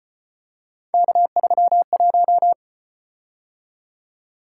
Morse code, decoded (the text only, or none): K31